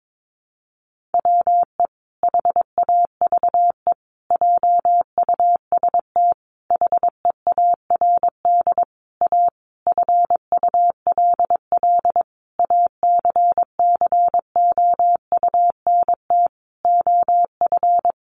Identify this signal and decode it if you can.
Morse code, heard as a keyed tone, about 22 wpm, 700 Hz. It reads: WE HAVE JUST HEARD A FULL ACCOUNT OF
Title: Sentences from Top 900 Words 22wpm